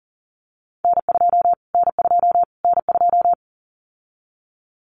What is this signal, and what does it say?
Morse code, heard as a keyed tone, about 40 wpm, 700 Hz. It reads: N2 N2 N2
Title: Callsign Prefixes for Top 25 DXCC - Repeated 3x - 40wpm